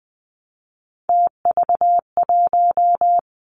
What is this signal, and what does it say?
Morse code, heard as a keyed tone, about 20 wpm, 700 Hz. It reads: TV1
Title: Three Challenging Letter-Number Combinations 20wpm